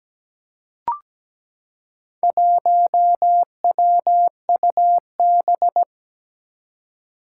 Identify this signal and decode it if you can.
Morse code, heard as a keyed tone, about 17 wpm, 700 Hz. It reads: 1WUB
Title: Four Challenging Letter-Number Combinations 17wpm